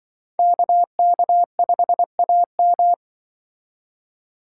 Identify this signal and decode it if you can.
Morse code, heard as a keyed tone, about 25 wpm, 700 Hz. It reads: KK5AM
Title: Speed Racing - USA Call Signs 20wpm